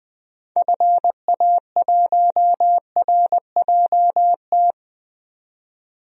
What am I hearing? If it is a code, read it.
Morse code, heard as a keyed tone, about 20 wpm, 700 Hz. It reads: FA1RJT
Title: DXCC-100 Call Signs - 6 Characters - 20wpm